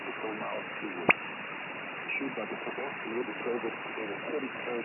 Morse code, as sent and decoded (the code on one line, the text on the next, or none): none